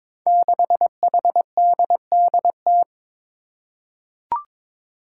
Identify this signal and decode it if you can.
Morse code, heard as a keyed tone, about 22 wpm, 700 Hz. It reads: 6HDDT
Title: Five Challenging Letter-Number Combinations 22wpm